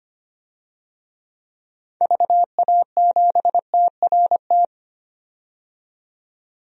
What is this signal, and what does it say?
Morse code, heard as a keyed tone, about 25 wpm, 700 Hz. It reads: VA7TRT
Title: DXCC-100 Call Signs - 6 Characters - 25wpm